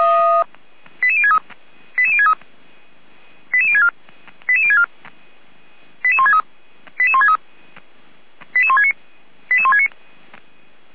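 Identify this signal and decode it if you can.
Five-tone selective call (ZVEI), heard as quick runs of five tones, tones of about 70 ms. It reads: 88063 88063 88064 88064 88163 88163 88169 88169
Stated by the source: ZVEI five-tone call received by SDR.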